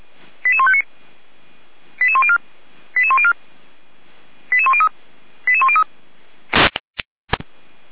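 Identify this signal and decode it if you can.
Five-tone selective call (ZVEI), heard as quick runs of five tones, tones of about 70 ms. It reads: 88159 88104 88104 88103 88103
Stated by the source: ZVEI five-tone call 88102 received by SDR.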